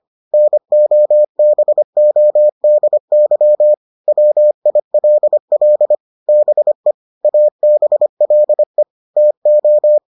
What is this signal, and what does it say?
Morse code, heard as a keyed tone, about 25 wpm, 600 Hz. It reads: NOBODY WILL BE ABLE TO